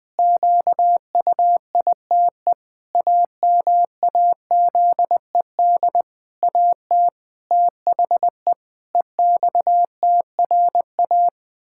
Morse code, heard as a keyed tone, about 20 wpm, 700 Hz. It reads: QUITE AMAZED AT THE EXTRA
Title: Pangram Challenge v2 - 20wpm